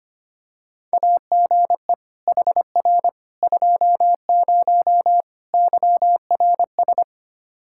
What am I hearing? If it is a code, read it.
Morse code, heard as a keyed tone, about 25 wpm, 700 Hz. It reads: AGE HR 20 YRS
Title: Practice QSO - Exchange 3 - More Info - 25wpm